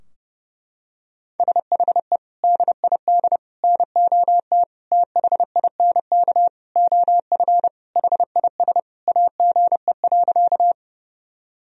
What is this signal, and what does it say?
Morse code, heard as a keyed tone, about 32 wpm, 700 Hz. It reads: SHE DID NOT THINK OF HIS AGE.
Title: Sentences from Top 1000 Words - Review Part 3 - 30wpm